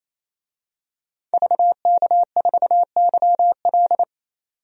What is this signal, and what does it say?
Morse code, heard as a keyed tone, about 27 wpm, 700 Hz. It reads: VK4YL
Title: DXCC-100 Call Signs - 5 Characters - 28wpm